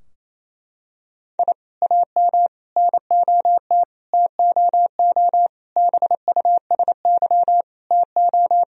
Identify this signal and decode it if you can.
Morse code, heard as a keyed tone, about 28 wpm, 700 Hz. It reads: I AM NOT TOO BUSY TO
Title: Sentences from Top 1000 Words - Review Part 3 - 28wpm